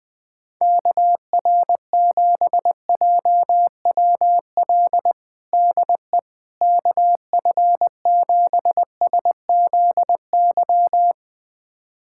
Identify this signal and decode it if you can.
Morse code, heard as a keyed tone, about 20 wpm, 700 Hz. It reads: KR7JWL DE KF7SZY
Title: USA Callsign DE Callsign - 6 Characters - 20wpm